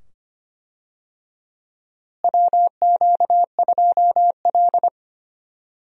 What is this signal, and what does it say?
Morse code, heard as a keyed tone, about 25 wpm, 700 Hz. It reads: WQ2L
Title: USA Call Signs - 4 Characters - Repeated 2x - 25wpm